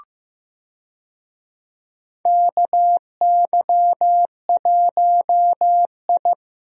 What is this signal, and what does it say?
Morse code, heard as a keyed tone, about 16 wpm, 700 Hz. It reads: KY1I
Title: USA Call Signs - 4 Characters - Repeated 2x - 15wpm